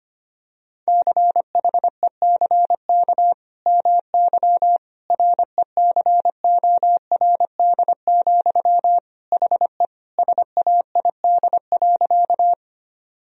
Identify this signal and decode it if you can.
Morse code, heard as a keyed tone, about 25 wpm, 700 Hz. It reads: CHECK MY RECORD, HE SAID.
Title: Sentences from Top 1000 Words - Review Part 3 - 25wpm